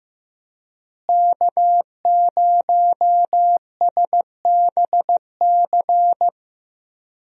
Morse code, heard as a keyed tone, about 15 wpm, 700 Hz. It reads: K0SBC
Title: USA Call Signs - 5 Characters - Repeated 3x - 15wpm